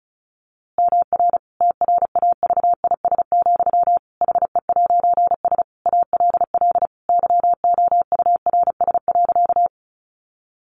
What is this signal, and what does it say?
Morse code, heard as a keyed tone, about 34 wpm, 700 Hz. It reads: MR TRAVIS, HE'S ALL YOURS.
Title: A Sound Of Thunder 35wpm